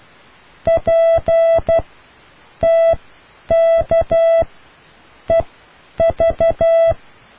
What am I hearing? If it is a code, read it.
Morse code, heard as a keyed tone, about 12 wpm, spaced as for 8 wpm, 660 Hz. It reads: PTKEV